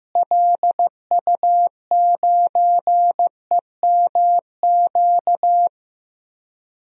Morse code, heard as a keyed tone, about 15 wpm, 700 Hz. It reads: LU9EMQ
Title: DXCC-100 Call Signs - 6 Characters - Repeated 2x - 15wpm